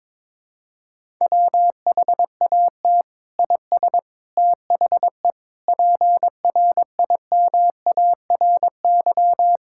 Morse code, heard as a keyed tone, about 22 wpm, 700 Hz. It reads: WHAT IS THE PRIMARY
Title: Sentences from Top 1000 Words - Review Part 5 - 22wpm